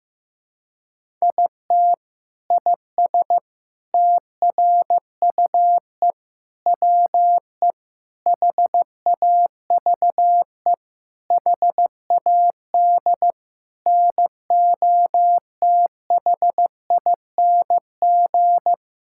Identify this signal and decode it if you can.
Morse code, heard as a keyed tone, about 15 wpm, 700 Hz. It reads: IT IS TRUE WE HAVE HAD NOTHING
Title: Sentences from Top 1000 Words - Review Part 5 - 15wpm